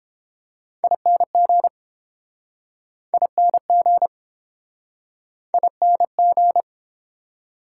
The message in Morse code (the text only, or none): ING ING ING